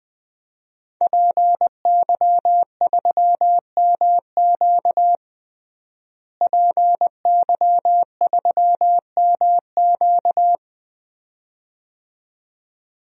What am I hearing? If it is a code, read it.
Morse code, heard as a keyed tone, about 20 wpm, 700 Hz. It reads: PY3MQ PY3MQ
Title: DXCC-100 Call Signs - 5 Characters - Repeated 2x - 20wpm